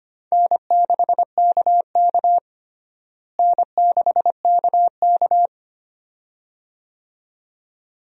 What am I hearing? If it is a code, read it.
Morse code, heard as a keyed tone, about 25 wpm, 700 Hz. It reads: N6KK N6KK
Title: USA Call Signs - 4 Characters - Repeated 2x - 25wpm